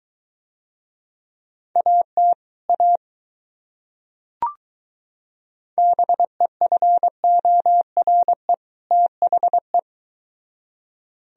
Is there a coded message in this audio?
Yes: Morse code from 1.8 to 9.8 s, heard as a keyed tone, about 23 wpm, 700 Hz. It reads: AT A BEFORE THE